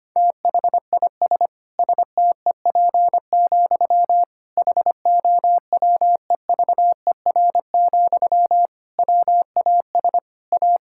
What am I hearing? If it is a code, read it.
Morse code, heard as a keyed tone, about 25 wpm, 700 Hz. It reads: THIS STEP, HOWEVER, WAS A
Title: Sentences from Top 1000 Words - Review Part 6 - 25wpm